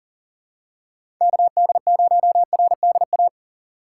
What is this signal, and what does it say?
Morse code, heard as a keyed tone, about 37 wpm, 700 Hz. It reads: KD0RDA
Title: USA Call Signs - 6 Characters - 6x Character Spacing - 40wpm